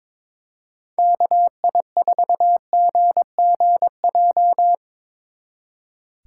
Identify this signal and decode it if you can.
Morse code, heard as a keyed tone, about 24 wpm, 700 Hz. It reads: KI4GGJ
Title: Speed Racing - USA Call Signs 22wpm